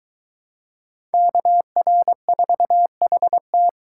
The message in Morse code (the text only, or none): KR4HT